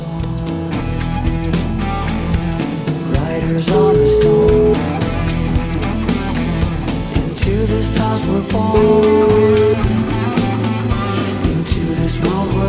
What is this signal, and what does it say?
Ringback tone (Europe, 425 Hz), with noise.